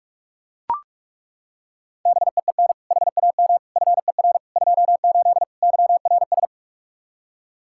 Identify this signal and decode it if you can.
Morse code, heard as a keyed tone, about 42 wpm, 700 Hz. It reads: BEEN HAM FER 28 YRS
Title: Practice QSO - Exchange 3 - More Info - 45wpm